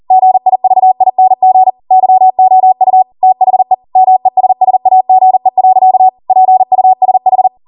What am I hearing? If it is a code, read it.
Morse code, heard as a keyed tone, about 40 wpm, 750 Hz. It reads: GIVING YOU THE MESSAGE. PUSH
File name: Morse / 40WPM